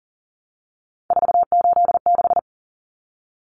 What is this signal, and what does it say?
Morse code, heard as a keyed tone, about 40 wpm, 700 Hz. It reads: 486
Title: Common Cut Numbers - 3-Digit 40wpm